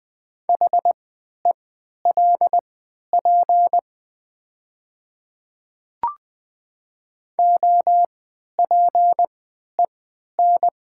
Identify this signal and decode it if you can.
Morse code, heard as a keyed tone, about 20 wpm, spaced as for 11 wpm, 700 Hz. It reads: HELP OPEN
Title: 3x Character Spacing - Top 300 Words - 20wpm